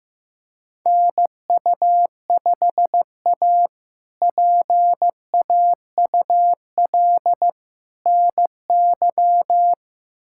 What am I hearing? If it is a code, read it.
Morse code, heard as a keyed tone, about 15 wpm, 700 Hz. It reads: NU5A PAUL NY
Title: K1USN SST QSO Practice - 15wpm